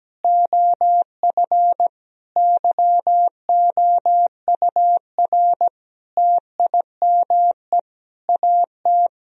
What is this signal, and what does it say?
Morse code, heard as a keyed tone, about 17 wpm, 700 Hz. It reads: OF YOUR TIME AT
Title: Sentences from Top 1000 Words - Review Part 1 - 17wpm